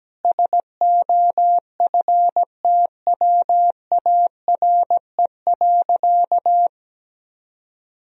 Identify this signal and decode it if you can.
Morse code, heard as a keyed tone, about 17 wpm, 700 Hz. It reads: SOFTWARE.
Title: Sentences from Top 1000 Words - Review Part 3 - 17wpm